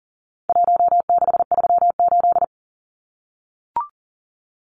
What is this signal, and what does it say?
Morse code, heard as a keyed tone, about 40 wpm, 700 Hz. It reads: 1638
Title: Numbers - Contest Style - 4-Digit 40wpm